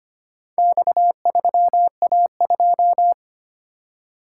Morse code, heard as a keyed tone, about 25 wpm, 700 Hz. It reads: X3A2
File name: practice / 25wpm / Four Letter-Number Combinations - 25wpm